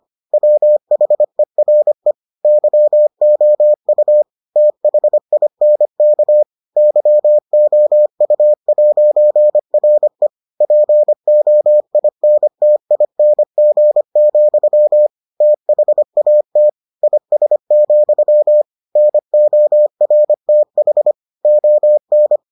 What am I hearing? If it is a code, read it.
Morse code, heard as a keyed tone, about 25 wpm, 600 Hz. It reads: WHERE YOU THINK YOU'RE POINTING, THAT IS, NORTH ON